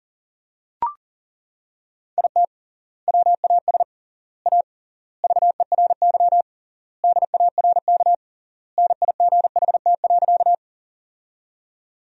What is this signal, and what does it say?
Morse code, heard as a keyed tone, about 40 wpm, 700 Hz. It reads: IT WAS A VERY DARK NIGHT.
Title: 3x Word-Spacing - Sentences from Top 500 Words - 40wpm